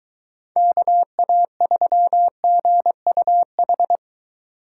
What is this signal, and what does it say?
Morse code, heard as a keyed tone, about 23 wpm, 700 Hz. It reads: KA3GUH